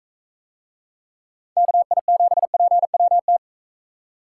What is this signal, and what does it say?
Morse code, heard as a keyed tone, about 45 wpm, 700 Hz. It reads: KI7PWT